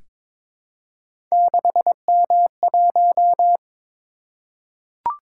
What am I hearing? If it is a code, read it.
Morse code, heard as a keyed tone, about 22 wpm, 700 Hz. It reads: 6M1
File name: practice / 22wpm / Three Letter-Number Combinations - 22wpm